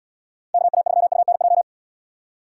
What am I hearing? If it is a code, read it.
Morse code, heard as a keyed tone, about 72 wpm, 700 Hz. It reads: KA3UNJ